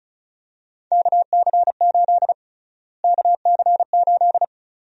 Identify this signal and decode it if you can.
Morse code, heard as a keyed tone, about 36 wpm, 700 Hz. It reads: KC8 KC8
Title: USA Call Sign Prefixes - Repeated 2x - 35wpm